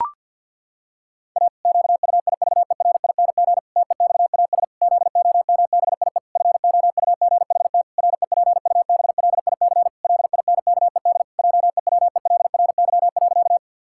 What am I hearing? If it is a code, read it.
Morse code, heard as a keyed tone, about 50 wpm, 700 Hz. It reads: A QUIVERING TEXAS ZOMBIE FOUGHT REPUBLIC LINKED JEWELRY.
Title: Pangram Challenge v2 - 50wpm